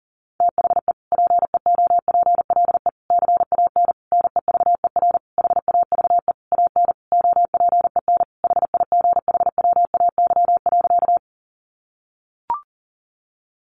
Morse code, heard as a keyed tone, about 40 wpm, 700 Hz. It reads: THE PEOPLE CAN NEVER HAVE AN OPEN HIGHWAY.
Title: Sentences from Top 1400 Words 40wpm